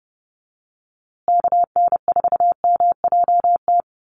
Morse code, heard as a keyed tone, about 30 wpm, 700 Hz. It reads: KN4MJT
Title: USA Call Signs Repeated 3x 30wpm